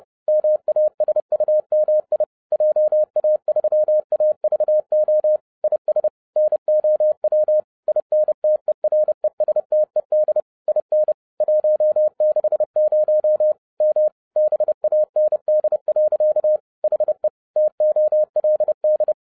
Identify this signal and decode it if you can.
Morse code, heard as a keyed tone, about 30 wpm, 610 Hz. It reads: MASUMI JA3AVO IS NOW INTERESTED IN 160 M BAND. HE TOLD